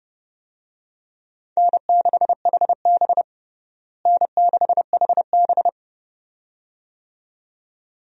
Morse code, heard as a keyed tone, about 30 wpm, 700 Hz. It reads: N6HB N6HB
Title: USA Call Signs - 4 Characters - Repeated 2x - 30wpm